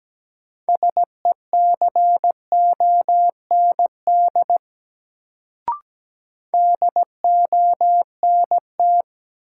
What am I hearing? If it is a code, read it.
Morse code, heard as a keyed tone, about 17 wpm, 700 Hz. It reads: SECOND DONT
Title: Sentences from Top 300 Words - New Word Drill - 17wpm